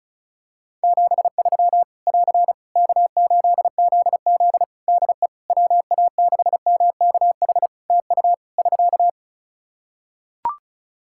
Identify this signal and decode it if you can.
Morse code, heard as a keyed tone, about 34 wpm, 700 Hz. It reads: 73 <AR> K8ZZ DE WA6MKH TU <SK>
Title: K1USN SST QSO Practice 35wpm